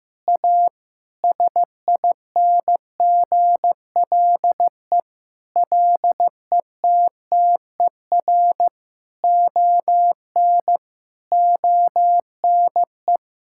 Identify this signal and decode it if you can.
Morse code, heard as a keyed tone, about 15 wpm, 700 Hz. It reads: A SINGLE LETTER ON ONE
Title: Sentences from Top 1600 Words 15wpm